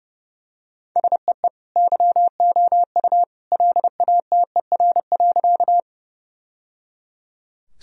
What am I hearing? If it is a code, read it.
Morse code, heard as a keyed tone, about 30 wpm, 700 Hz. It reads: SEE YOU LATER.
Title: Sentences from Top 1000 Words - Review Part 4 - 30wpm